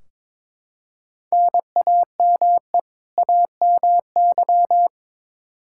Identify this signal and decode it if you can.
Morse code, heard as a keyed tone, about 22 wpm, 700 Hz. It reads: NAME AMY
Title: Practice QSO - Exchange 1 - Beginning - 22wpm